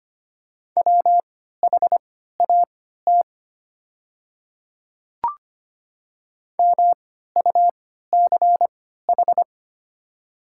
Morse code, heard as a keyed tone, about 25 wpm, spaced as for 14 wpm, 700 Hz. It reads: WHAT MUCH
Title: 3x Character Spacing - Top 200 Words - 25wpm